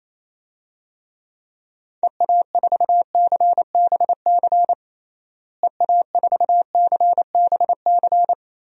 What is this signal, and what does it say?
Morse code, heard as a keyed tone, about 28 wpm, 700 Hz. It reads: EA4CBC EA4CBC
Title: DXCC-100 Call Signs - 6 Characters - Repeated 2x - 28wpm